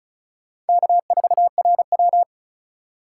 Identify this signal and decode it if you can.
Morse code, heard as a keyed tone, about 34 wpm, 700 Hz. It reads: K4RW
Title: USA Call Signs - 4 Characters - 3x Character Spacing - 35wpm